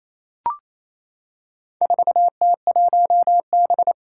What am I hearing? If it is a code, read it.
Morse code, heard as a keyed tone, about 28 wpm, 700 Hz. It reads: 4T1B
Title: Four Challenging Letter-Number Combinations 28wpm